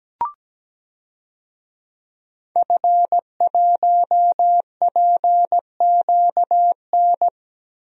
Morse code, heard as a keyed tone, about 17 wpm, 700 Hz. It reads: F1PQN
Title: DXCC-100 Call Signs - 5 Characters - Repeated 3x - 17wpm